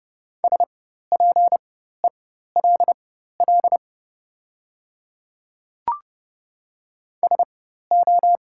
Morse code, heard as a keyed tone, about 30 wpm, spaced as for 14 wpm, 690 Hz. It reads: SPELL SO
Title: 4x Character Spacing - Top 200 Words - 30wpm